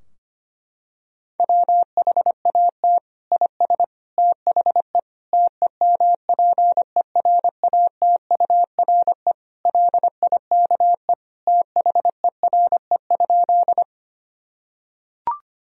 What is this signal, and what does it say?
Morse code, heard as a keyed tone, about 25 wpm, 700 Hz. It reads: WHAT IS THE TEMPERATURE LIKE THERE?
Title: Sentences from Top 1000 Words - Review Part 4 - 25wpm